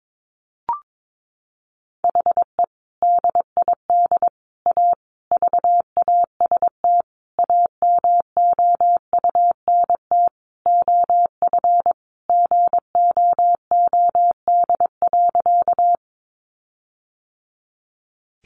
Morse code, heard as a keyed tone, about 22 wpm, 700 Hz. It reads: HE DID A VAST AMOUNT OF GOOD.
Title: Sentences from Top 1300 Words 22wpm